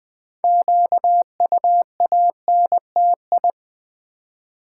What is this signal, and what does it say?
Morse code, heard as a keyed tone, about 20 wpm, 700 Hz. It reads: QUANTI
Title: Build A Word - 3 Letters At A Time - 8-Letter Words - 20wpm